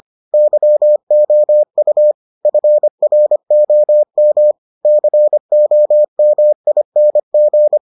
Morse code, heard as a keyed tone, about 25 wpm, 600 Hz. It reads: YOU FROM COMING